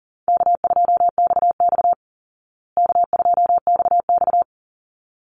Morse code, heard as a keyed tone, about 40 wpm, 700 Hz. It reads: K2XX K2XX
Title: Easy Call Signs Repeated 2x - 40wpm